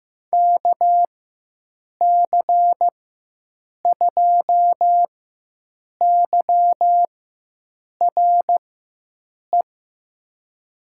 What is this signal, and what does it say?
Morse code, heard as a keyed tone, about 15 wpm, spaced as for 7 wpm, 700 Hz. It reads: KC2YRE
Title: USA Call Signs - 6 Characters - 4x Character Spacing - 15wpm